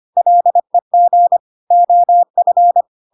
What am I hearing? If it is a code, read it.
Morse code, heard as a keyed tone, about 25 wpm, 700 Hz. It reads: LEG OF
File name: Pinocchio / Pinocchio ch1 25 25 ik1hge